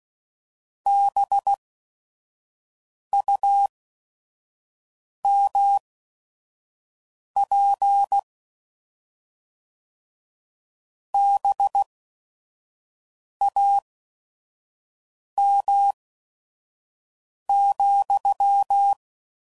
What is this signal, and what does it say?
Morse code, heard as a keyed tone, about 16 wpm, spaced as for 5 wpm, 800 Hz. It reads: BUMP BAM,